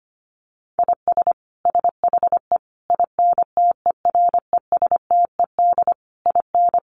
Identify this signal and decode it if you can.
Morse code, heard as a keyed tone, about 25 wpm, 700 Hz. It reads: IS SHE INTERESTED IN
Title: Sentences from Top 900 Words 25wpm